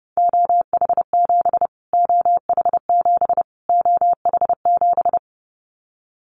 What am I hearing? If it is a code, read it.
Morse code, heard as a keyed tone, about 30 wpm, 700 Hz. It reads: OH7 OH7 OH7
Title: Callsign Prefixes for Top 25 DXCC - Repeated 3x - 30wpm